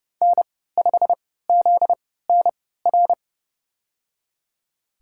Morse code, heard as a keyed tone, about 32 wpm, spaced as for 17 wpm, 700 Hz. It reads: N5ZNR
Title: USA Call Signs - 5 Characters - 3x Character Spacing - 30wpm